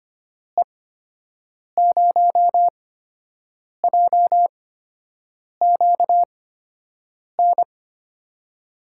Morse code, heard as a keyed tone, about 25 wpm, spaced as for 7 wpm, 700 Hz. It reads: E0JQN
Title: USA Call Signs - 6 Characters - 8x Character Spacing - 25wpm